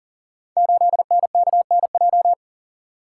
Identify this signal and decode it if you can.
Morse code, heard as a keyed tone, about 39 wpm, 700 Hz. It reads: 8NKNJ
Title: Code Groups - 5 Letter:Numbers - 40wpm